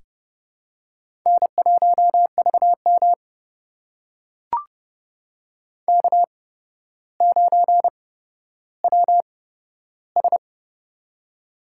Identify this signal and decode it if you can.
Morse code, heard as a keyed tone, about 30 wpm, spaced as for 13 wpm, 700 Hz. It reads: N1VM K9WS
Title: USA Call Signs - 4 Characters - 8x Character Spacing - 30wpm